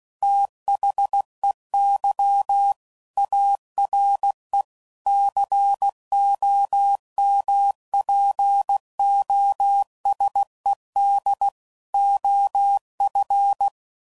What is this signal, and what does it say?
Morse code, heard as a keyed tone, about 16 wpm, 800 Hz. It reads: THEY ARE COMPOSED OF
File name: CW Code Groups @15WPM/Technical Subjects